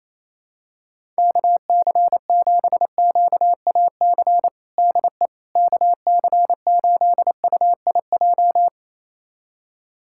Morse code, heard as a keyed tone, about 28 wpm, 700 Hz. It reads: KC7QAC DE KC8UIJ
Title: USA Callsign DE Callsign - 6 Characters - 28wpm